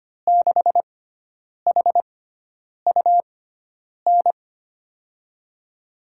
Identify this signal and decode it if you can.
Morse code, heard as a keyed tone, about 25 wpm, spaced as for 9 wpm, 700 Hz. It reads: 6HUN
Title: USA Call Signs - 6 Characters - 6x Character Spacing - 25wpm